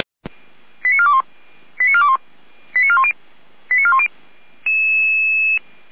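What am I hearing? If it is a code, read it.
Five-tone selective call (ZVEI), heard as quick runs of five tones, tones of about 70 ms. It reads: 89421 89421 89410 89410
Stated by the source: ZVEI five-tone selective call received on an SDR, from the Feuerwehr Orlamünde audio archive.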